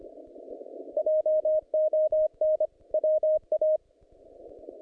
Morse code, heard as a keyed tone, about 25 wpm, 620 Hz. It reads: JON WA
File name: contests / naqp-cw-k5tr-op-k5pi- -0100